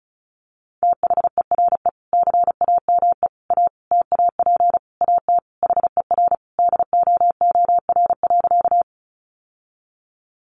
Morse code, heard as a keyed tone, about 34 wpm, 700 Hz. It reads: THERE CAME A TAP AT HER DOOR.
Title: Sentences from Top 1400 Words 35wpm